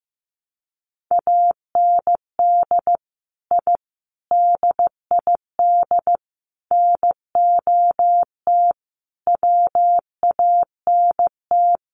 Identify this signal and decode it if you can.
Morse code, heard as a keyed tone, about 15 wpm, 700 Hz. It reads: AND I DID NOT WANT
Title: Sets of 5 Words 15wpm